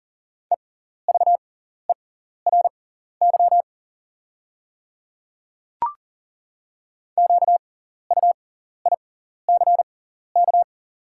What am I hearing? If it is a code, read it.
Morse code, heard as a keyed tone, about 40 wpm, spaced as for 14 wpm, 690 Hz. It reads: EVERY QUICK
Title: 6x Character Spacing - Top 500 Words - 40wpm